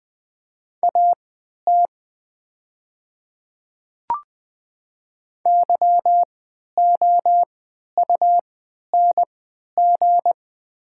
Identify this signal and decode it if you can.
Morse code, heard as a keyed tone, about 20 wpm, spaced as for 11 wpm, 700 Hz. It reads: AT YOUNG
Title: 3x Character Spacing - Top 500 Words - 20wpm